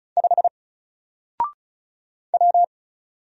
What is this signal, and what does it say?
Morse code, heard as a keyed tone, about 35 wpm, 700 Hz. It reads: 5 W
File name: Single-Letter-Number-35wpm